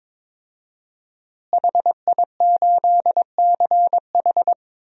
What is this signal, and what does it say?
Morse code, heard as a keyed tone, about 22 wpm, 700 Hz. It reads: HI8CH
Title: DXCC-100 Call Signs - 5 Characters - Repeated 3x - 22wpm